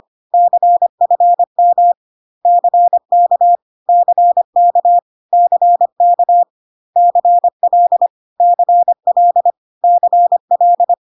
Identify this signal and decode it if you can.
Morse code, heard as a keyed tone, about 25 wpm, 700 Hz. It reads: CFM CK CK CK CL CL CL
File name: CW abbreviations alphabetical repeated 3 times 25 25wpm